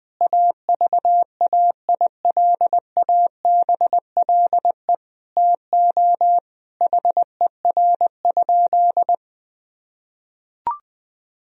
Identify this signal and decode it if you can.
Morse code, heard as a keyed tone, about 20 wpm, 700 Hz. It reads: AVAILABLE TO HER?
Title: Sentences from Top 1000 Words - Review Part 3 - 20wpm